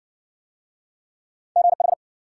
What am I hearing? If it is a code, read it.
Morse code, heard as a keyed tone, about 53 wpm, 700 Hz. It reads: GH